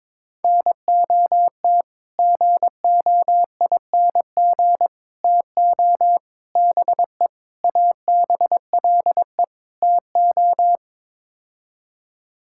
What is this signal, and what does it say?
Morse code, heard as a keyed tone, about 22 wpm, 700 Hz. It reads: NOT GOING TO BE ABLE TO
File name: practice / 22wpm / Sets of 6 Words - 22wpm